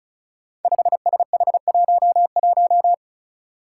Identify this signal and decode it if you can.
Morse code, heard as a keyed tone, about 36 wpm, 700 Hz. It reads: 5SH11
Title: Five Challenging Letter-Number Combinations 35wpm